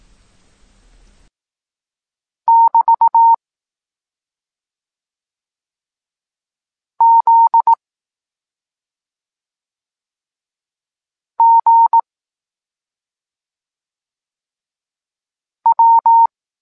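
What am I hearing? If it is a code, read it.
Morse code, heard as a keyed tone, about 18 wpm, 920 Hz. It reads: <BT> Z G W
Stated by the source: Evenly spaced Morse lesson text 21 for learning letters G and Z, from SP9MOA Niepołomicki Klub Krótkofalowców.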